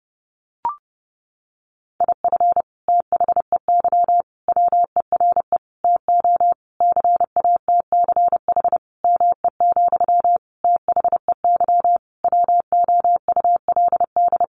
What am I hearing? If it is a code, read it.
Morse code, heard as a keyed tone, about 30 wpm, 700 Hz. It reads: IF THEY WERE TO CATCH ME, THEY WOULD
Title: Sentences from Top 1500 Words 30wpm